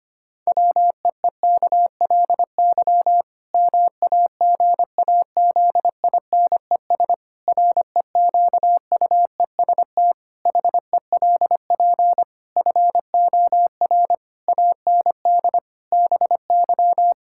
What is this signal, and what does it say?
Morse code, heard as a keyed tone, about 25 wpm, 700 Hz. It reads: WEEKLY MAGAZINES REQUEST HELP FOR AND BY
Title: Pangram Challenge v2 - 25wpm